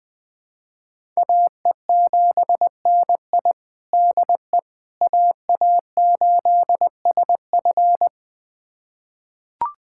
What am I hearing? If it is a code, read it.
Morse code, heard as a keyed tone, about 21 wpm, 700 Hz. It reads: AE7NI DE AA8SF
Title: USA Callsign DE Callsign - 5 Characters - 20wpm